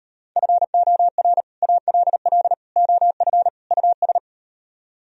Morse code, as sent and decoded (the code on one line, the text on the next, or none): ..-. --- .-. / .- .-.. .-.. / --- ..-. / ..- ...
FOR ALL OF US